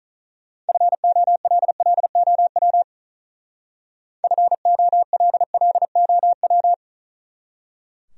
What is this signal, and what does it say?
Morse code, heard as a keyed tone, about 40 wpm, 700 Hz. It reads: FOLLOW FOLLOW
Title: Speed-Racing Top 200 Words 35wpm